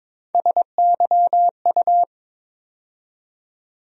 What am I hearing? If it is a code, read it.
Morse code, heard as a keyed tone, about 22 wpm, 700 Hz. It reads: SYU
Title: USA Call Sign Postfixes - 3 Letters 22wpm